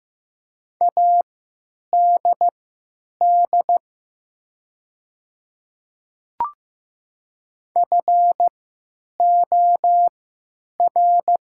Morse code, heard as a keyed tone, about 15 wpm, spaced as for 9 wpm, 700 Hz. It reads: ADD FOR
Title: 3x Character Spacing - Top 300 Words - 15wpm